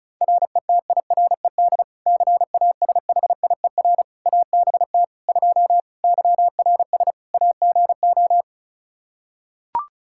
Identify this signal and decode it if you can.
Morse code, heard as a keyed tone, about 36 wpm, 700 Hz. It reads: RETIRED CASHIER ABT 2 YRS AGO
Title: K1USN SST QSO Practice 35wpm